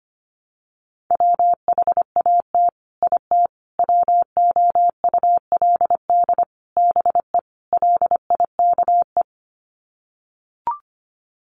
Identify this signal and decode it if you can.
Morse code, heard as a keyed tone, about 25 wpm, 700 Hz. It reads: WHAT IT WOULD BE LIKE
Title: Sets of 5 Words Encore 25wpm